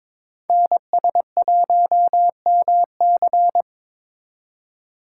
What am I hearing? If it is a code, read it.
Morse code, heard as a keyed tone, about 22 wpm, 700 Hz. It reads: NS1MC
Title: USA Call Signs - 5 Characters - 3x Character Spacing - 22wpm